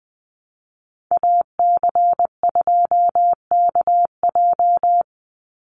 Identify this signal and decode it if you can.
Morse code, heard as a keyed tone, about 20 wpm, 700 Hz. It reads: AC2KJ
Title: Call Signs Repeated 2x - 20wpm